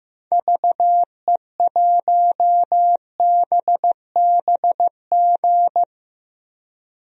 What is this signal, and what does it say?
Morse code, heard as a keyed tone, about 16 wpm, 700 Hz. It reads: VE1BBG
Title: DXCC-100 Call Signs - 6 Characters - Repeated 3x - 15wpm